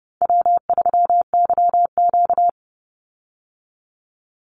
Morse code, heard as a keyed tone, about 30 wpm, 700 Hz. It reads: W3YQ
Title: Call Signs Repeated 2x - 30wpm